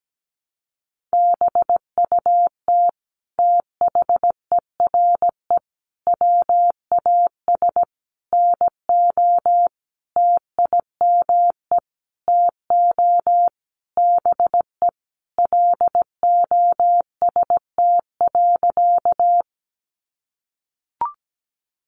Morse code, heard as a keyed tone, about 17 wpm, 700 Hz. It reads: BUT THERE WAS NO TIME TO BE LOST.
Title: Sentences from Top 1400 Words 17wpm